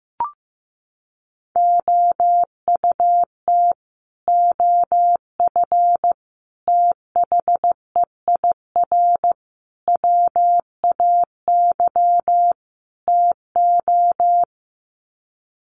Morse code, heard as a keyed tone, about 15 wpm, 700 Hz. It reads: OUT OF THEIR WAY TO
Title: Sets of 5 Words Encore2 15wpm